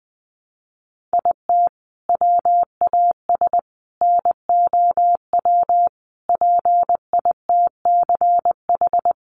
Morse code, heard as a keyed tone, about 20 wpm, 700 Hz. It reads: IT WAS NOW PITCH
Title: Sentences from Top 1300 Words 20wpm